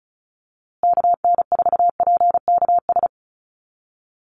Morse code, heard as a keyed tone, about 34 wpm, 700 Hz. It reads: KN4PKS
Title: USA Call Signs Repeated 2x 35wpm